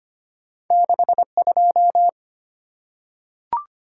Morse code, heard as a keyed tone, about 25 wpm, 700 Hz. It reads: 62
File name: Two Letter-Number Combinations - 25wpm